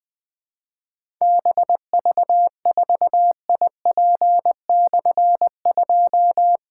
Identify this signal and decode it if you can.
Morse code, heard as a keyed tone, about 20 wpm, 700 Hz. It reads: BV4IP/2
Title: Difficult Call Signs 20wpm